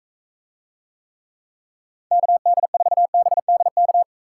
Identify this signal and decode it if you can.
Morse code, heard as a keyed tone, about 44 wpm, 700 Hz. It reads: KD4BDK